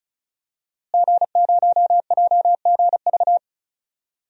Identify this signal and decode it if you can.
Morse code, heard as a keyed tone, about 36 wpm, 700 Hz. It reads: G0JGV
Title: DXCC-100 Call Signs - 5 Characters - Repeated 3x - 35wpm